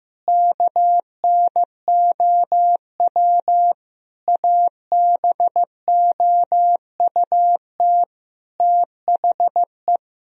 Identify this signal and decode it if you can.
Morse code, heard as a keyed tone, about 16 wpm, 700 Hz. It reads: KNOW ABOUT THE
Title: Sentences from Top 1000 Words - Review Part 4 - 15wpm